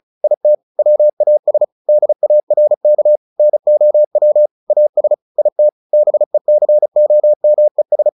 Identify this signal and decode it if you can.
Morse code, heard as a keyed tone, about 34 wpm, 600 Hz. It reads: IT WAS DARK NOW AS IT BECOMES
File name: the old man and the sea 2 35wpm